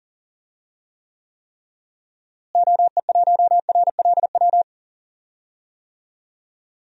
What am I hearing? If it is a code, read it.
Morse code, heard as a keyed tone, about 40 wpm, 700 Hz. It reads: OE1RLW
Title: DXCC-100 Call Signs - 6 Characters - Repeated 3x - 40wpm